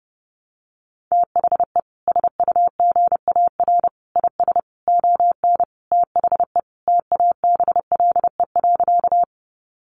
Morse code, heard as a keyed tone, about 30 wpm, 700 Hz. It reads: THE SUGAR IS ON THE TABLE.
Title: Sentences from Top 1200 Words 30wpm